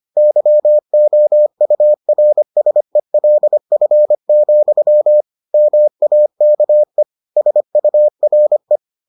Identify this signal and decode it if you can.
Morse code, heard as a keyed tone, about 25 wpm, 600 Hz. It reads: YOURSELF, MAKE SURE